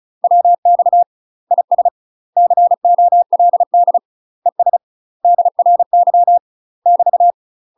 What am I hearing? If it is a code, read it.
Morse code, heard as a keyed tone, about 36 wpm, 700 Hz. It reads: WX IS COLD ES DRY <BT>
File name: UK MASTER QSOs TRAINING / 35WPM / UK MASTER QSOs 0028